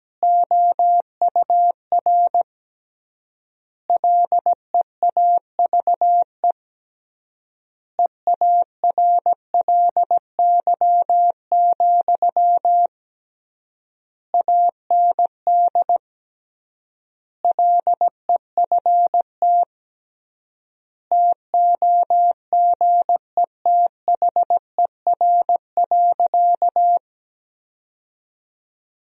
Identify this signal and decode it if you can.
Morse code, heard as a keyed tone, about 17 wpm, 700 Hz. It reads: OUR LEAVE EARLY, AND LEFT TOGETHER.
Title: 3x Word-Spacing - Sentences from Top 400 Words - 17wpm